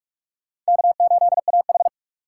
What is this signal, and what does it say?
Morse code, heard as a keyed tone, about 56 wpm, 700 Hz. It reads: K8AH
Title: Speed Racing - USA 1x2 Call Signs 45wpm